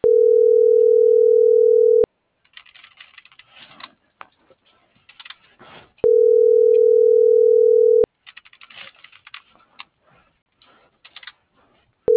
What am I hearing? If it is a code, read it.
Ringback tone (North America), 440 + 480 Hz.